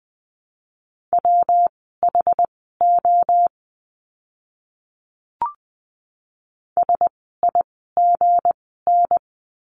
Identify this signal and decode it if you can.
Morse code, heard as a keyed tone, about 20 wpm, spaced as for 15 wpm, 700 Hz. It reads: WHO SIGN
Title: 160 Lesson 26f - Letter g - Words 20wpm